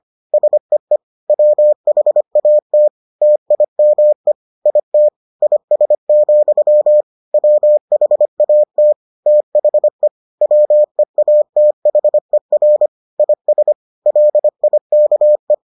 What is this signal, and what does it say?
Morse code, heard as a keyed tone, about 25 wpm, 600 Hz. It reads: SEE WHAT TIME IT IS, WHAT THE WEATHER IS LIKE